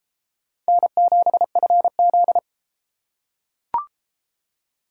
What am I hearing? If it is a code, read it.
Morse code, heard as a keyed tone, about 30 wpm, 700 Hz. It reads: N7FZ